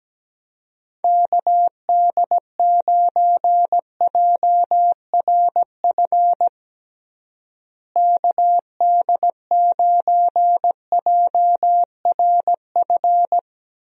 Morse code, heard as a keyed tone, about 17 wpm, 700 Hz. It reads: KD9JRF KD9JRF
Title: USA Call Signs - 6 Characters - Repeated 3x - 17wpm